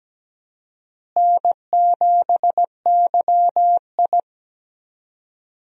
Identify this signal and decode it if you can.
Morse code, heard as a keyed tone, about 17 wpm, 700 Hz. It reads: N7YI
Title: USA Call Signs - 4 Characters - Repeated 2x - 17wpm